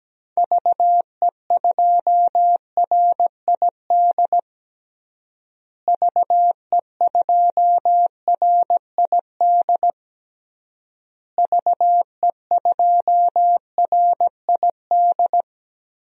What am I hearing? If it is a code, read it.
Morse code, heard as a keyed tone, about 17 wpm, 700 Hz. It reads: VE2RID VE2RID VE2RID
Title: DXCC-100 Call Signs - 6 Characters - Repeated 3x - 17wpm